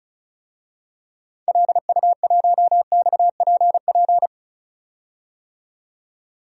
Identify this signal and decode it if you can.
Morse code, heard as a keyed tone, about 36 wpm, 700 Hz. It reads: LU1XPP
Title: DXCC-100 Call Signs - 6 Characters - 35wpm